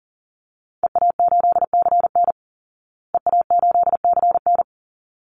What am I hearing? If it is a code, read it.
Morse code, heard as a keyed tone, about 40 wpm, 700 Hz. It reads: EA8CN EA8CN
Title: Call Signs Repeated 3x - 40wpm